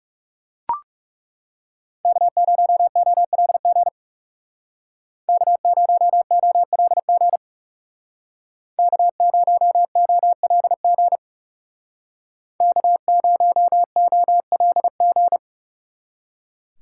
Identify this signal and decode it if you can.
Morse code, heard as a keyed tone, about 38 wpm, 700 Hz. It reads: K0OLG K0OLG K0OLG K0OLG